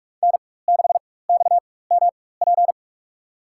Morse code, heard as a keyed tone, about 45 wpm, spaced as for 21 wpm, 700 Hz. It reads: N6XMP
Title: USA Call Signs - 4x Character Spacing - 45wpm